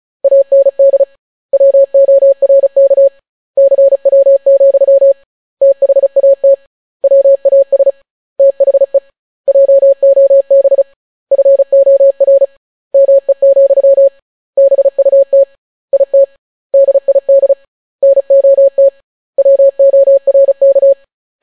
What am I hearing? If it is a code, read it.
Morse code, heard as a keyed tone, about 36 wpm, 550 Hz. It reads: AND WORK CW, THAT WAS THE JOB FOR ME, BUT IT DID NOT WORK